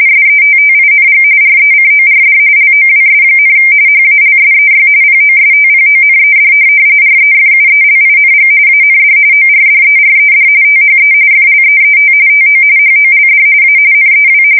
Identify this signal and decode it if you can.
RTTY, about 45.45 baud. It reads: SETTING RECORD LOWS, HALSEY POSTPONED REFUELING AND ORDERED THE FLEET TO HOLD A NEW
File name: fsk170 / USS-New-Jersey